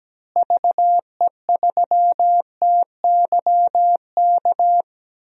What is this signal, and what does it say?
Morse code, heard as a keyed tone, about 17 wpm, 700 Hz. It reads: VE3TYK
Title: DXCC-100 Call Signs - 6 Characters - Repeated 3x - 17wpm